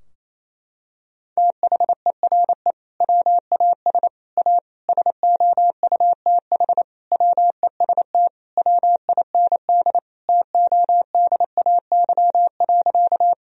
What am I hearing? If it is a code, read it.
Morse code, heard as a keyed tone, about 28 wpm, 700 Hz. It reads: THERE WAS A SOUTH WEST WIND TODAY.
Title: Sentences from Top 1000 Words - Review Part 2 - 28wpm